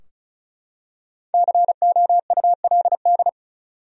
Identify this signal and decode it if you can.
Morse code, heard as a keyed tone, about 34 wpm, 700 Hz. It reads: COULD
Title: Top 100 Words Repeated 35wpm